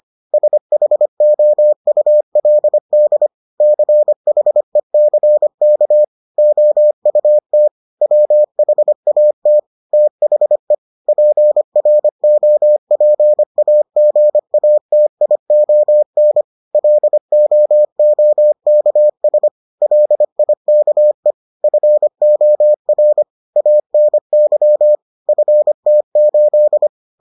Morse code, heard as a keyed tone, about 25 wpm, 600 Hz. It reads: SHOULD CHECK OUT WHAT THE PROPAGATION LOOKS LIKE FOR ANY FT8